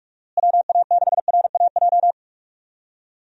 Morse code, heard as a keyed tone, about 41 wpm, 700 Hz. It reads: WA6RAJ
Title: USA Call Signs - 6 Characters - Repeated 2x - 45wpm